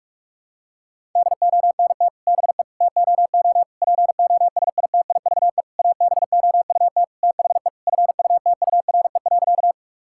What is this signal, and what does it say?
Morse code, heard as a keyed tone, about 46 wpm, 700 Hz. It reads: DONT BE TOO POSITIVE ABOUT THE FUTURE.
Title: Sentences from Top 1000 Words - Review Part 5 - 45wpm